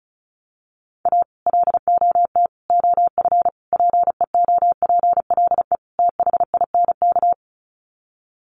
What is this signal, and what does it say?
Morse code, heard as a keyed tone, about 34 wpm, 700 Hz. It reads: A LOT OF PEOPLE THINK
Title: Sets of 5 Words Encore 35wpm